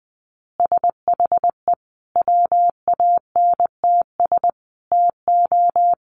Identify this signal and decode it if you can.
Morse code, heard as a keyed tone, about 20 wpm, 700 Hz. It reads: SHE WANTS TO
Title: Easy Phrases 20wpm